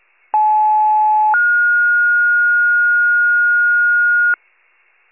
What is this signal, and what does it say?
Two-tone page, heard as a one-second tone, then a three-second tone, about 860 then 1430 Hz, with noise. It is heard from 0.3 to 4.3 s.
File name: pager / 75 FIRE / wcalert